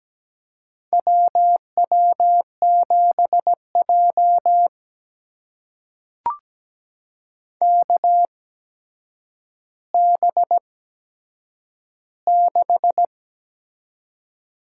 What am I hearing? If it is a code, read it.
Morse code, heard as a keyed tone, about 17 wpm, spaced as for 8 wpm, 700 Hz. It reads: WW7J KB6
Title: USA Call Signs - 8x Character Spacing - 17wpm